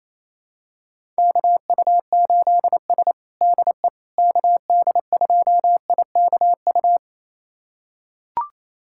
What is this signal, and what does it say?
Morse code, heard as a keyed tone, about 28 wpm, 700 Hz. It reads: KU8S DE KD2IKU
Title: USA Callsign DE Callsign 28wpm